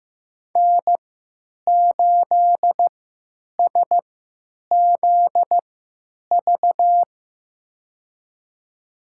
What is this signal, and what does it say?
Morse code, heard as a keyed tone, about 15 wpm, spaced as for 9 wpm, 700 Hz. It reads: N8SZV
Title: USA Call Signs - 5 Characters - 3x Character Spacing - 15wpm